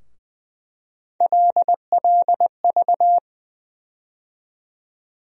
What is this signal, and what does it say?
Morse code, heard as a keyed tone, about 20 wpm, 700 Hz. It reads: LLV
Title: USA Call Sign Postfixes - 3 Letters 20wpm